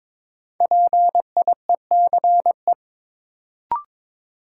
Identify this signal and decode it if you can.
Morse code, heard as a keyed tone, about 22 wpm, 700 Hz. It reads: PIECE
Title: Sets of 3 Words - New Word Drill - 22wpm